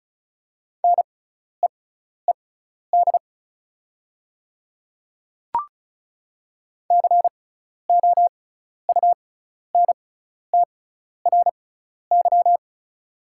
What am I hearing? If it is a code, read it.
Morse code, heard as a keyed tone, about 34 wpm, spaced as for 12 wpm, 690 Hz. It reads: NEED COUNTRY
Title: 6x Character Spacing - Top 500 Words - 35wpm